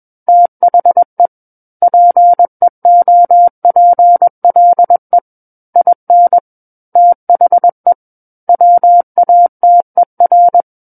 Morse code, heard as a keyed tone, about 21 wpm, 700 Hz. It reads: THE PEOPLE IN THE WATER
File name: Book One / 12-Chapter 12 - 21 WPM